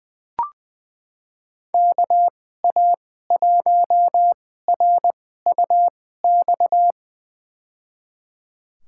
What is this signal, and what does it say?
Morse code, heard as a keyed tone, about 20 wpm, spaced as for 14 wpm, 700 Hz. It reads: KA1RUX
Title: USA Call Signs - 2x Character Spacing - 20wpm